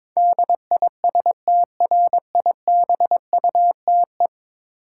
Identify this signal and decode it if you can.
Morse code, heard as a keyed tone, about 22 wpm, 700 Hz. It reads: DISTRIBUTE
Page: morse-code-ninja-practice-22wpm